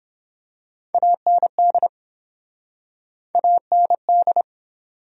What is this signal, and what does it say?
Morse code, heard as a keyed tone, about 29 wpm, 700 Hz. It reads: AND AND